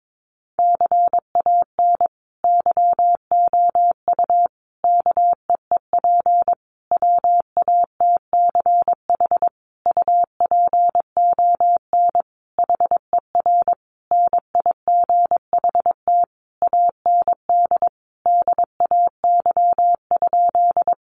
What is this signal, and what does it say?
Morse code, heard as a keyed tone, about 22 wpm, 700 Hz. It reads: CAN YOU KEEP WATCH UPON HER NIGHT AND DAY?
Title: Sentences from Top 1100 Words 22wpm